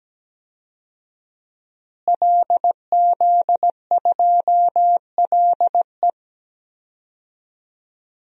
Morse code, heard as a keyed tone, about 17 wpm, 700 Hz. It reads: LZ2LE
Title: DXCC-100 Call Signs - 5 Characters - Repeated 2x - 17wpm